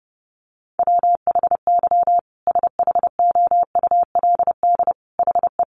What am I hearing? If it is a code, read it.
Morse code, heard as a keyed tone, about 28 wpm, 700 Hz. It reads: WHY SHOULD HE
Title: Sentences from Top 1300 Words 30wpm